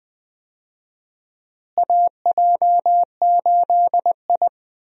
Morse code, heard as a keyed tone, about 21 wpm, 700 Hz. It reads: AJ8I
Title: USA Call Signs - 4 Characters - Repeated 2x - 20wpm